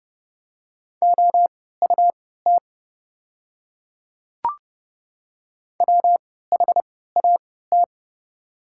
Morse code, heard as a keyed tone, about 30 wpm, spaced as for 17 wpm, 700 Hz. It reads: OUT WHAT
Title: 3x Character Spacing - Top 500 Words - 30wpm